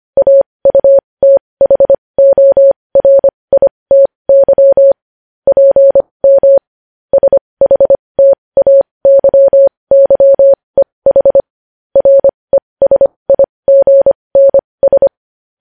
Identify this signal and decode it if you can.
Morse code, heard as a keyed tone, about 25 wpm, spaced as for 20 wpm, 550 Hz. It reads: AUTHORITY PM SHTAYYEH RESIGNS
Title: News Headlines at 20